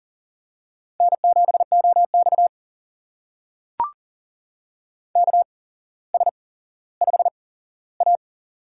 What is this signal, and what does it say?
Morse code, heard as a keyed tone, about 40 wpm, spaced as for 17 wpm, 700 Hz. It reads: N7OX KS5A
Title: USA Call Signs - 4 Characters - 8x Character Spacing - 40wpm